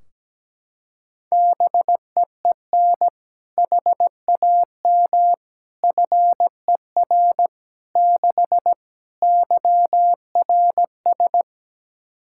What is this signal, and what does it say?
Morse code, heard as a keyed tone, about 17 wpm, 700 Hz. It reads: BEEN HAM FER 6 YRS
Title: K1USN SST QSO Practice 17wpm